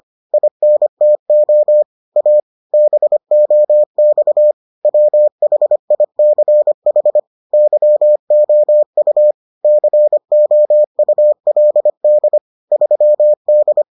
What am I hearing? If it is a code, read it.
Morse code, heard as a keyed tone, about 25 wpm, 600 Hz. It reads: INTO A BOX WHICH YOU COULD 3D